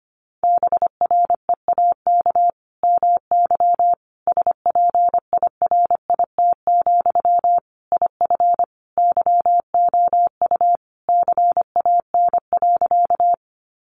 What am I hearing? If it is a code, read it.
Morse code, heard as a keyed tone, about 25 wpm, 700 Hz. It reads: BREAK MY SPIRIT, IF YOU CAN.
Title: Sentences from Top 1400 Words 25wpm